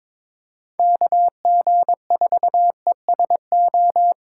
Morse code, heard as a keyed tone, about 25 wpm, 700 Hz. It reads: KG4ESO
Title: Speed Racing - USA 2x3 Call Signs 22wpm